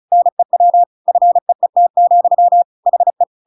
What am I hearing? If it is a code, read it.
Morse code, heard as a keyed tone, about 34 wpm, 700 Hz. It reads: NEW FEET, HE
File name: Pinocchio / Pinocchio ch8 35 35 ik1hge